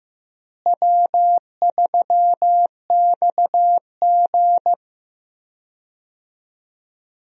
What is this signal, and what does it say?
Morse code, heard as a keyed tone, about 15 wpm, 700 Hz. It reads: W3XG
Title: USA Call Signs - 4 Characters - Repeated 2x - 15wpm